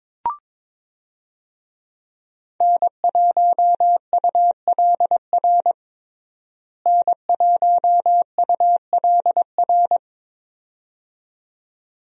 Morse code, heard as a keyed tone, about 22 wpm, 700 Hz. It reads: N1ULR N1ULR
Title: USA Call Signs - 5 Characters - Repeated 2x - 22wpm